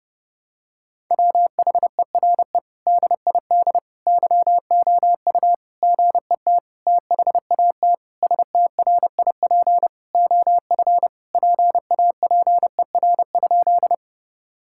Morse code, heard as a keyed tone, about 30 wpm, 700 Hz. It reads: WHERE DID YOU GET THAT STRIP OF PAPER?
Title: Sentences from Top 1600 Words 30wpm